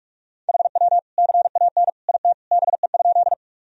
Morse code, heard as a keyed tone, about 46 wpm, 700 Hz. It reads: HW CAN IT BE?
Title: Sentences from Top 100 Words and Abbreviations 45wpm